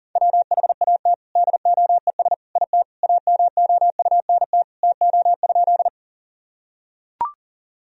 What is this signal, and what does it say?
Morse code, heard as a keyed tone, about 40 wpm, 700 Hz. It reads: WHAT DOES IT AMOUNT TO?
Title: Sentences from Top 1000 Words - Review Part 4 - 40wpm